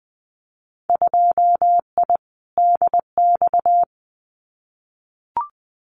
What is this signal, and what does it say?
Morse code, heard as a keyed tone, about 20 wpm, 700 Hz. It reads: 2I DX
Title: Winter Field Day Exchanges 20wpm